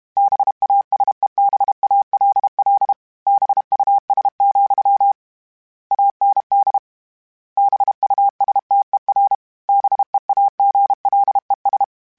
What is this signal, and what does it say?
Morse code, heard as a keyed tone, about 32 wpm, 800 Hz. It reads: BASEBALL BUS, AND BUSTER BEAGLES